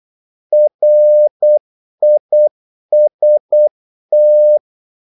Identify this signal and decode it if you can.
Morse code, heard as a keyed tone, about 8 wpm, 600 Hz. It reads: RIST
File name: Korte ord DK 8wpm